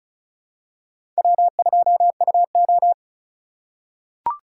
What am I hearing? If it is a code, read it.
Morse code, heard as a keyed tone, about 34 wpm, 700 Hz. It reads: W2UO
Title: USA Call Signs - 2x Character Spacing - 35wpm